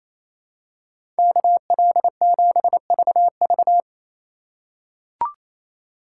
Morse code, heard as a keyed tone, about 28 wpm, 700 Hz. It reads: KL7VV
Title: USA Call Signs - 5 Characters - 4x Character Spacing - 28wpm